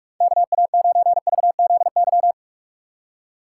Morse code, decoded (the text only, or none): KA9VZY